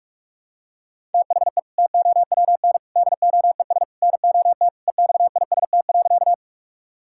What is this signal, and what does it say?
Morse code, heard as a keyed tone, about 46 wpm, 700 Hz. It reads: THE TOWN DOES NOT EXIST.
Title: Sentences from Top 1000 Words - Review Part 5 - 45wpm